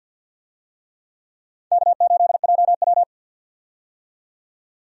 Morse code, heard as a keyed tone, about 49 wpm, 700 Hz. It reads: K8JW
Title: USA Call Signs - 4 Characters - 50wpm